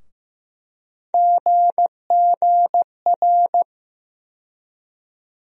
Morse code, heard as a keyed tone, about 15 wpm, 700 Hz. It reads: GGR
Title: USA Call Sign Postfixes - 3 Letters 15wpm